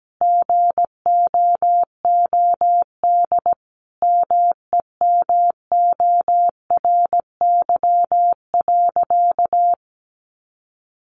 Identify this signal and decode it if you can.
Morse code, heard as a keyed tone, about 17 wpm, 700 Hz. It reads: GOOD MEMORY.
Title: Sentences from Top 1300 Words 17wpm